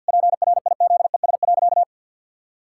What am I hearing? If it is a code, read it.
Morse code, heard as a keyed tone, about 50 wpm, 700 Hz. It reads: PRIZES.
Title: Pangram Challenge v2 - 50wpm